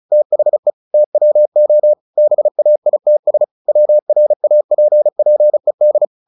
Morse code, heard as a keyed tone, about 34 wpm, 600 Hz. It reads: THE TWO BAITS WRAPPED
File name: the old man and the sea / the old man and the sea 1 35wpm